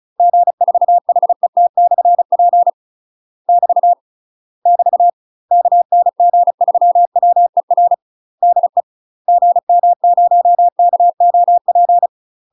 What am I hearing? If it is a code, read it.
Morse code, heard as a keyed tone, about 36 wpm, 700 Hz. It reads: G4HET/P <BT> <BT> KN G3WER DE GM0KOP
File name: UK MASTER QSOs TRAINING / 35WPM / UK MASTER QSOs 0014